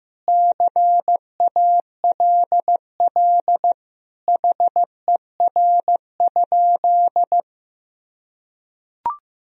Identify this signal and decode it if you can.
Morse code, heard as a keyed tone, about 16 wpm, 700 Hz. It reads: CALL HER?
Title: Sentences from Top 100 Words and Abbreviations 15wpm